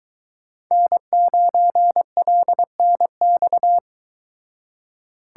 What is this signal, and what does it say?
Morse code, heard as a keyed tone, about 23 wpm, 700 Hz. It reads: N9LNX